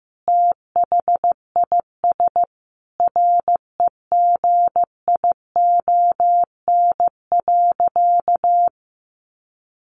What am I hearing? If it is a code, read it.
Morse code, heard as a keyed tone, about 15 wpm, 700 Hz. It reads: THIS REGION.
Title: Sentences from Top 1500 Words 15wpm